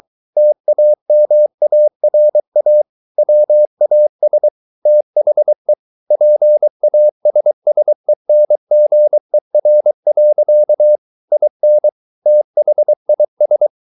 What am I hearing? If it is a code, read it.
Morse code, heard as a keyed tone, about 23 wpm, 600 Hz. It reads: TAMARA WAS THE PASSENGER. IN THIS
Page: Romanen Never af Ken Follet på engelsk.